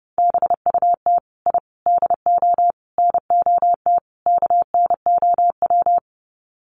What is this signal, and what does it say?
Morse code, heard as a keyed tone, about 30 wpm, 700 Hz. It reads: BUT I DO NOT KNOW
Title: Sets of 5 Words 30wpm